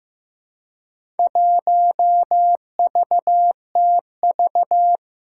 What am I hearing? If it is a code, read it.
Morse code, heard as a keyed tone, about 16 wpm, 700 Hz. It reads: 1VTV
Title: Four Challenging Letter-Number Combinations 15wpm